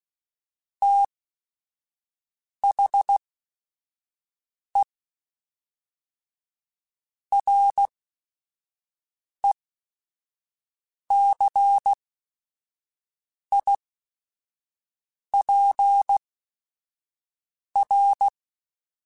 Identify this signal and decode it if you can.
Morse code, heard as a keyed tone, about 16 wpm, spaced as for 5 wpm, 800 Hz. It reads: THE RECIPR
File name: Technical Subjects / Technical Subjects-mp3s / XC @5